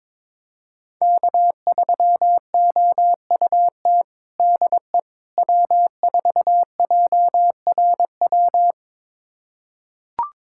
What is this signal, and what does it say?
Morse code, heard as a keyed tone, about 22 wpm, 700 Hz. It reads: K3OUT DE W4JRW
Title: USA Callsign DE Callsign - 5 Characters - 22wpm